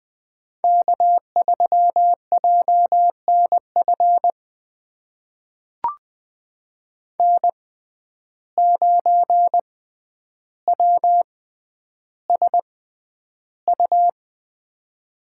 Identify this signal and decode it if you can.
Morse code, heard as a keyed tone, about 20 wpm, spaced as for 10 wpm, 700 Hz. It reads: K3JNF N9WSU
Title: USA Call Signs - 5 Characters - 6x Character Spacing - 20wpm